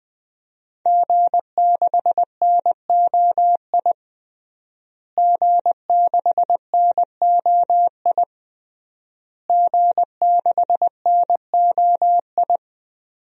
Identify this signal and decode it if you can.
Morse code, heard as a keyed tone, about 20 wpm, 700 Hz. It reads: G6NOI G6NOI G6NOI
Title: DXCC-100 Call Signs - 5 Characters - Repeated 3x - 20wpm